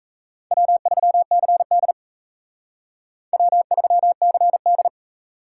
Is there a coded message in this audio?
Yes: Morse code from 0.5 to 4.9 s, heard as a keyed tone, about 35 wpm, 700 Hz. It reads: W3CD W3CD